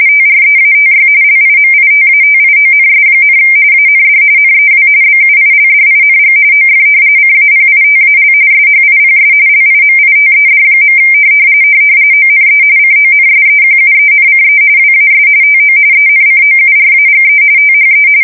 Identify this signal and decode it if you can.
RTTY, about 45.45 baud. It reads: TREATY LIMITATIONS, AND NEW BUILDING OF CAPITAL SHIPS LIMITED. ONE CONSEQUENCE WAS TO ENCOURAGE THE
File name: fsk170 / usnavy4